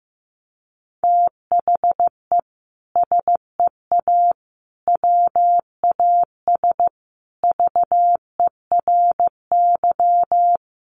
Morse code, heard as a keyed tone, about 15 wpm, 700 Hz. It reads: THE SEA WAS VERY
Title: Sentences from Top 1400 Words 15wpm